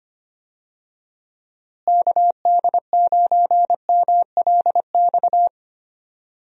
Morse code, heard as a keyed tone, about 25 wpm, 700 Hz. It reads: KD9MLX
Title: USA Call Signs - 6 Characters - Repeated 3x - 25wpm